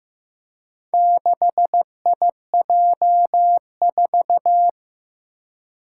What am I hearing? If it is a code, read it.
Morse code, heard as a keyed tone, about 15 wpm, 700 Hz. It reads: 6IJ4
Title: Four Challenging Letter-Number Combinations 15wpm